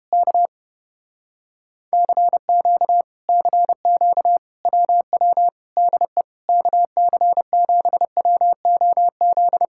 Morse code, heard as a keyed tone, about 31 wpm, 700 Hz. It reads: K CQ CQ WW DE KC7WOZ